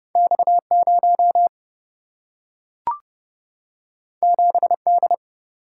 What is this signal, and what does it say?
Morse code, heard as a keyed tone, about 30 wpm, 700 Hz. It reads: X 0 7 D
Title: Single-Letter Single-Number 30wpm